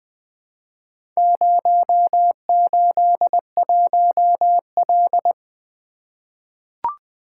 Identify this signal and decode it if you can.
Morse code, heard as a keyed tone, about 20 wpm, 700 Hz. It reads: 081L
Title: Four Letter-Number Combinations - 20wpm